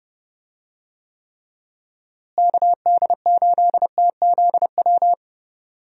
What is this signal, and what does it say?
Morse code, heard as a keyed tone, about 29 wpm, 700 Hz. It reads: KD8TZW
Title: USA Call Signs - 6 Characters - Repeated 3x - 30wpm